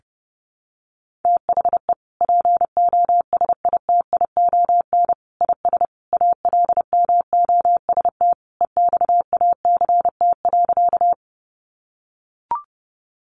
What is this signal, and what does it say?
Morse code, heard as a keyed tone, about 30 wpm, 700 Hz. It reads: THE POSITION IS ALMOST EXACT.
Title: Sentences from Top 1300 Words 30wpm